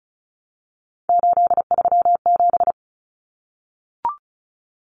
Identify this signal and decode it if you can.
Morse code, heard as a keyed tone, about 34 wpm, 700 Hz. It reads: 837
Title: Numbers - 3-Digit 35wpm